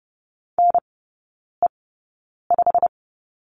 Morse code, heard as a keyed tone, about 30 wpm, spaced as for 9 wpm, 700 Hz. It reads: NE5
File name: Lesson 18e - Letter m - Three Characters 30wpm